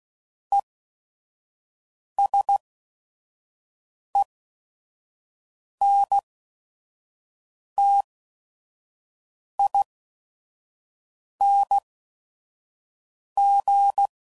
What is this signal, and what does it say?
Morse code, heard as a keyed tone, about 16 wpm, spaced as for 5 wpm, 800 Hz. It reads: ESENTING
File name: CW Code Groups @5WPM / Technical Subjects / XC